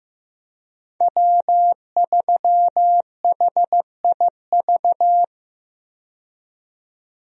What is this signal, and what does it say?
Morse code, heard as a keyed tone, about 15 wpm, 700 Hz. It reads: W3HIV
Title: USA Call Signs - 5 Characters - 15wpm